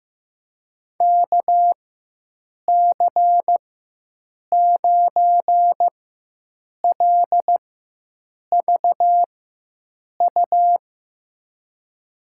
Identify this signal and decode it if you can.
Morse code, heard as a keyed tone, about 15 wpm, spaced as for 7 wpm, 700 Hz. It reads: KC9LVU
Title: USA Call Signs - 6 Characters - 4x Character Spacing - 15wpm